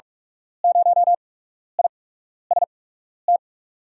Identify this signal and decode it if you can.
Morse code, heard as a keyed tone, about 46 wpm, spaced as for 12 wpm, 700 Hz. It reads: 0IST
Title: USA Call Signs - 5 Characters - 8x Character Spacing - 45wpm